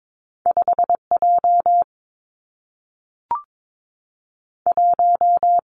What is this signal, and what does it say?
Morse code, heard as a keyed tone, about 22 wpm, 700 Hz. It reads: 5 J 1
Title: Two Challenging Letter-Number Combinations 22wpm